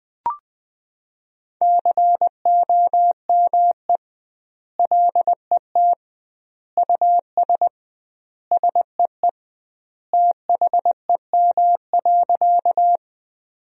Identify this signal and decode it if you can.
Morse code, heard as a keyed tone, about 20 wpm, 700 Hz. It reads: COME LET US SEE THEM.
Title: 2x Word-Spacing - Sentences from Top 300 Words - 20wpm